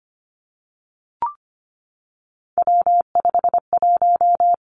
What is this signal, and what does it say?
Morse code, heard as a keyed tone, about 25 wpm, 700 Hz. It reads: W51
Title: Three Challenging Letter-Number Combinations 25wpm